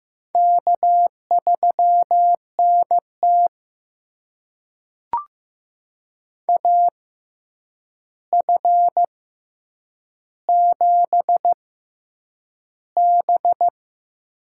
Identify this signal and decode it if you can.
Morse code, heard as a keyed tone, about 15 wpm, spaced as for 8 wpm, 700 Hz. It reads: K3NT AF7B
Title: USA Call Signs - 4 Characters - 6x Character Spacing - 15wpm